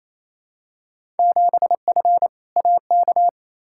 Morse code, heard as a keyed tone, about 28 wpm, 700 Hz. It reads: 7F AK
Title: Field Day Exchanges Encore - 28wpm